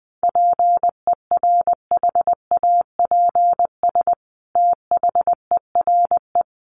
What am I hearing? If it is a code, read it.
Morse code, heard as a keyed tone, about 19 wpm, 700 Hz. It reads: PERHAPS THERE
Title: Sentences from Top 600 Words 20wpm